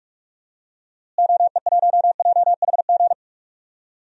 Morse code, heard as a keyed tone, about 46 wpm, 700 Hz. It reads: OE1JHG
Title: DXCC-100 Call Signs - 6 Characters - Repeated 3x - 45wpm